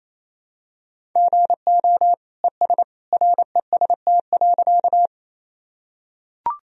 Morse code, heard as a keyed tone, about 28 wpm, 700 Hz. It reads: GO ES REST.
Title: Sentences from Top 500 Words and Abbreviations 28wpm